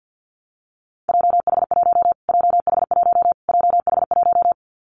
Morse code, heard as a keyed tone, about 50 wpm, 700 Hz. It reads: JH1 JH1 JH1
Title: Callsign Prefixes for Top 25 DXCC - Repeated 3x - 50wpm